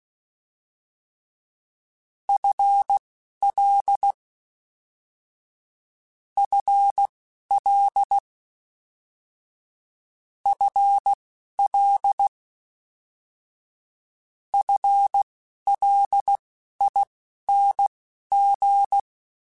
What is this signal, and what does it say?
Morse code, heard as a keyed tone, about 16 wpm, spaced as for 12 wpm, 800 Hz. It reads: FL FL FL FLING
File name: FL / FL mp3s / FL Prefixes 3x2 Twice @10